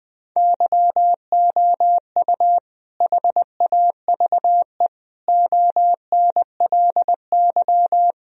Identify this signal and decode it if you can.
Morse code, heard as a keyed tone, about 20 wpm, 700 Hz. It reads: YOU HAVE ONLY
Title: Sentences from Top 1000 Words - Review Part 2 - 20wpm